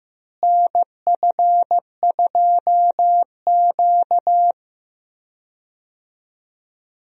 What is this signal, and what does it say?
Morse code, heard as a keyed tone, about 15 wpm, 700 Hz. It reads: NF2Q
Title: USA Call Signs - 4 Characters - Repeated 3x - 15wpm